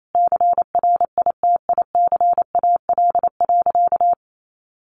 Morse code, heard as a keyed tone, about 28 wpm, 700 Hz. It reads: CRITICAL.
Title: Sentences from Top 1100 Words 28wpm